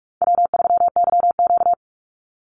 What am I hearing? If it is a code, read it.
Morse code, heard as a keyed tone, about 44 wpm, 700 Hz. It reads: W3YQ
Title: Call Signs Repeated 2x - 45wpm